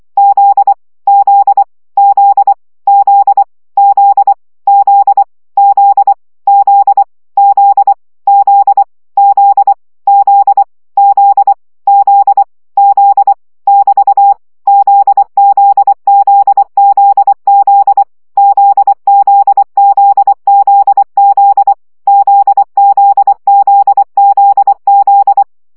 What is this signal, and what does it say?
Morse code, heard as a keyed tone, about 24 wpm, 800 Hz. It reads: Z Z Z Z Z Z Z Z Z Z Z Z Z Z Z <BT> ZZZZZ ZZZZZ ZZZZZ
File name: CW gyakorlas z 20 baud